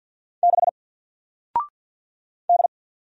Morse code, heard as a keyed tone, about 50 wpm, 700 Hz. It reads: 6 D
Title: Single Letter-Number Alt 50wpm